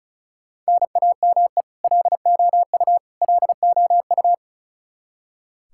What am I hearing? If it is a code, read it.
Morse code, heard as a keyed tone, about 36 wpm, 700 Hz. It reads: NAME LOU LOU
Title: Practice QSO - Exchange 1 - Beginning - 35wpm